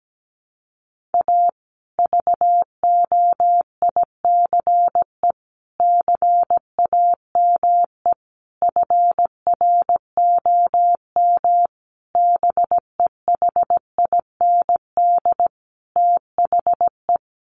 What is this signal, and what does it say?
Morse code, heard as a keyed tone, about 17 wpm, 700 Hz. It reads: A VOICE CAME FROM BEHIND THE
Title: Sentences from Top 1400 Words 17wpm